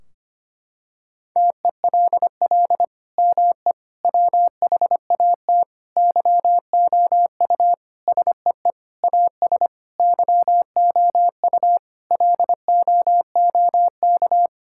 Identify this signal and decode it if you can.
Morse code, heard as a keyed tone, about 25 wpm, 700 Hz. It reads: TELL ME WHAT YOU SEE AS YOU LOOK
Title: Sentences from Top 1600 Words 25wpm